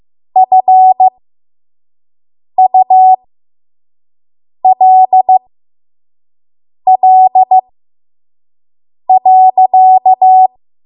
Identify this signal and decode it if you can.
Morse code, heard as a keyed tone, about 15 wpm, spaced as for 5 wpm, 750 Hz. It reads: FULL.
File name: Archive / 5 WPM / 05WPM